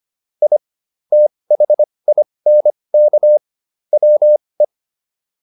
Morse code, heard as a keyed tone, about 25 wpm, spaced as for 20 wpm, 600 Hz. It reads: I THINK WE